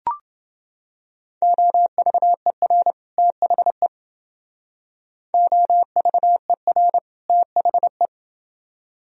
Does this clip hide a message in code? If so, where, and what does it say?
Yes: Morse code from 1.4 to 8.1 s, heard as a keyed tone, about 30 wpm, 700 Hz. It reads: OVER THE OVER THE